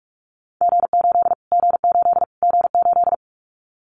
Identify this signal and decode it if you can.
Morse code, heard as a keyed tone, about 44 wpm, 700 Hz. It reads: G8 G8 G8
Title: Callsign Prefixes for Top 25 DXCC - Repeated 3x - 45wpm